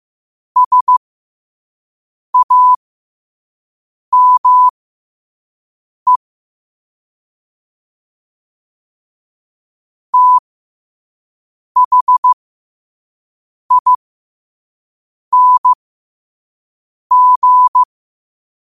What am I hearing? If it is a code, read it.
Morse code, heard as a keyed tone, about 15 wpm, spaced as for 5 wpm, 1000 Hz. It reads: SAME THING
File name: qotdfiles / 5wpm